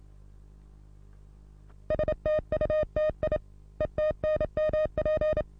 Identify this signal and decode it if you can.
Morse code, heard as a keyed tone, about 37 wpm, 620 Hz. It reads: STUTI ETNMP
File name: repos / Morsekurs / DJ4UF Morsen27